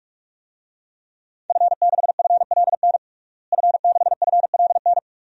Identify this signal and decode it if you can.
Morse code, heard as a keyed tone, about 46 wpm, 700 Hz. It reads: F6FLN F6FLN
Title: DXCC-100 Call Signs - 5 Characters - Repeated 3x - 45wpm